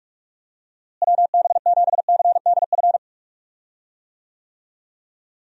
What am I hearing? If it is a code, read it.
Morse code, heard as a keyed tone, about 46 wpm, 700 Hz. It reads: WB7CDF
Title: USA Call Signs - 6 Characters - Repeated 3x - 45wpm